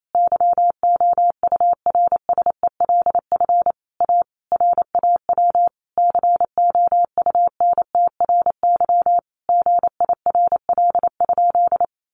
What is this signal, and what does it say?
Morse code, heard as a keyed tone, about 28 wpm, 700 Hz. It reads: YOURSELF A RAW COUNTRY GIRL?
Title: Sentences from Top 1500 Words 28wpm